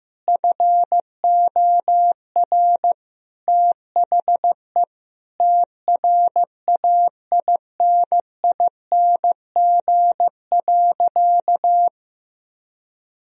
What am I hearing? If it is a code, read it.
Morse code, heard as a keyed tone, about 15 wpm, 700 Hz. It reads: FOR THE TRAINING.
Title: Sentences from Top 1000 Words - Review Part 4 - 15wpm